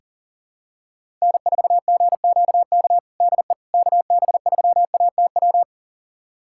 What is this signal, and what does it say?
Morse code, heard as a keyed tone, about 40 wpm, 700 Hz. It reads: N4GQK DE KB3ATW
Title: USA Callsign DE Callsign 40wpm